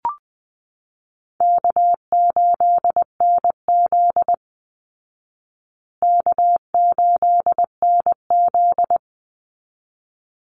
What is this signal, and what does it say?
Morse code, heard as a keyed tone, about 20 wpm, 700 Hz. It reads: K8NZ K8NZ
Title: Call Signs Repeated 3x - 20wpm